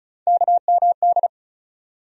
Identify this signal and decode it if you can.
Morse code, heard as a keyed tone, about 34 wpm, 700 Hz. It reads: KMD
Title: USA Call Sign Postfixes - 3 Letters 35wpm